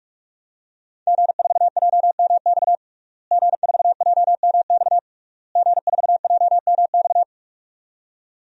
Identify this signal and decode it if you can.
Morse code, heard as a keyed tone, about 47 wpm, 700 Hz. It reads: G4JMX G4JMX G4JMX
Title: DXCC-100 Call Signs - 5 Characters - Repeated 3x - 45wpm